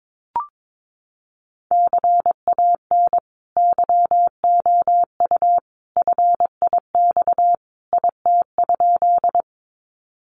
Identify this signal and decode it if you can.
Morse code, heard as a keyed tone, about 22 wpm, 700 Hz. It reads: CAN YOU FIX IT?
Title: Sentences from Top 1400 Words 22wpm